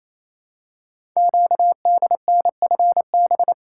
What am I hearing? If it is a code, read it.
Morse code, heard as a keyed tone, about 29 wpm, 700 Hz. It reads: QDNFB
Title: Code Groups - 5 Letters - 28wpm